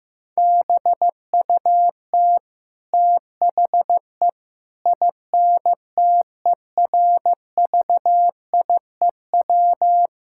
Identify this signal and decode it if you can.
Morse code, heard as a keyed tone, about 16 wpm, 700 Hz. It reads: BUT THE INTERVIEW
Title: Sentences from Top 1600 Words 15wpm